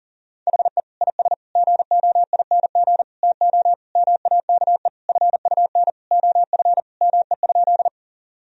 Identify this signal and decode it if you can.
Morse code, heard as a keyed tone, about 40 wpm, 700 Hz. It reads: HE IS GOING TO MAKE FUN OF ME?
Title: Sentences from Top 1000 Words - Review Part 4 - 40wpm